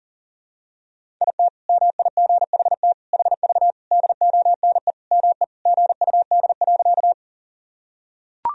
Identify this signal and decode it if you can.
Morse code, heard as a keyed tone, about 40 wpm, 700 Hz. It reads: IT MIGHT HV DONE ME GUD.
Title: Sentences from Top 500 Words and Abbreviations 40wpm